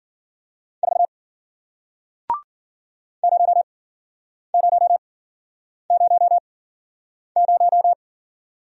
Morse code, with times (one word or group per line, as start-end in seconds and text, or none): 0.8-1.1 4
3.2-7.9 0000